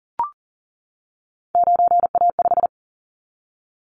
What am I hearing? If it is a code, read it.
Morse code, heard as a keyed tone, about 40 wpm, 700 Hz. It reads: 9A5
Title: Callsign Prefixes for Top 25 DXCC - 40wpm